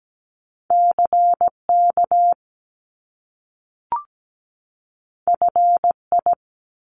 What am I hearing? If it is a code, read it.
Morse code, heard as a keyed tone, about 17 wpm, 700 Hz. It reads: CK FI
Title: Top Two Letter Combinations in Words 17wpm